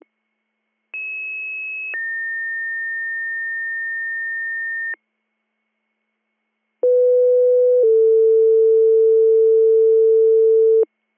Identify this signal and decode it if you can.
Two-tone page, heard as a one-second tone, then a three-second tone, about 2600 then 1800 Hz, about 500 then 450 Hz. It is heard from 0.9 to 10.8 s.